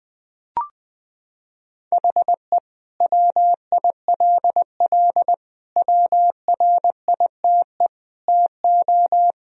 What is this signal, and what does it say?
Morse code, heard as a keyed tone, about 21 wpm, 700 Hz. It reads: HE WILL WRITE TO
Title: Sentences from Top 1000 Words - Review Part 1 - 20wpm